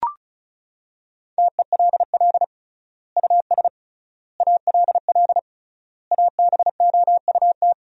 Morse code, heard as a keyed tone, about 36 wpm, 700 Hz. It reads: TELL US ALL ABOUT
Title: 3x Word-Spacing - Sentences from Top 200 Words - 35wpm